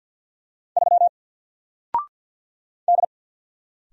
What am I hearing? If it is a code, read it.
Morse code, heard as a keyed tone, about 50 wpm, 700 Hz. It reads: 3 D
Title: Single Letter-Number 50wpm